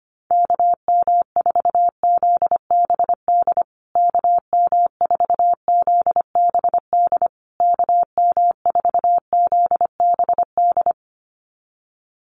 Morse code, heard as a keyed tone, about 25 wpm, 700 Hz. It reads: KM4ZBD KM4ZBD KM4ZBD
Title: USA Call Signs Repeated 3x 25wpm